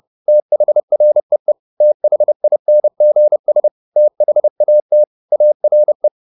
Morse code, heard as a keyed tone, about 30 wpm, 600 Hz. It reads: THREE THINGS THAT ARE
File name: the old man and the sea / the old man and the sea 1 30wpm